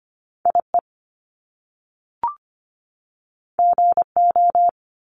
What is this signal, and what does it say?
Morse code, heard as a keyed tone, about 25 wpm, 700 Hz. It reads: IE GO
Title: Top Two Letter Combinations in Words 25wpm